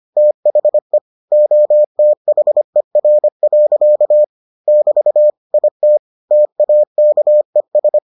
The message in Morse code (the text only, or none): THE OTHER. <BT> IT TAKES